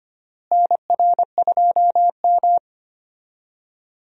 Morse code, heard as a keyed tone, about 25 wpm, 700 Hz. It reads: NR2M
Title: USA Call Signs - 4 Characters - Repeated 3x - 25wpm